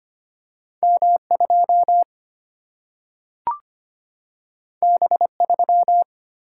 Morse code, heard as a keyed tone, about 25 wpm, 700 Hz. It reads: M 2 B 3
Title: Single-Letter Single-Number 25wpm